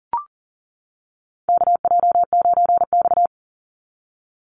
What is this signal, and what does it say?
Morse code, heard as a keyed tone, about 40 wpm, 700 Hz. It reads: KJ9X
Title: NA SOTA Chaser Callsigns 40wpm